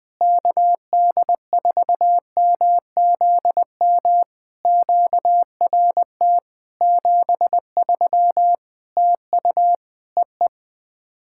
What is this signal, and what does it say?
Morse code, heard as a keyed tone, about 20 wpm, 700 Hz. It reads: KD4MZM QRT 73 TU EE
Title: POTA QSO Practice - 20wpm